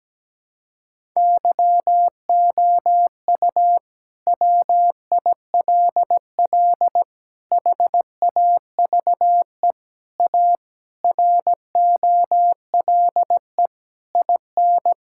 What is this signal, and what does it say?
Morse code, heard as a keyed tone, about 17 wpm, 700 Hz. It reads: YOU WILL HAVE A ROLE IN
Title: Sentences from Top 1000 Words - Review Part 6 - 17wpm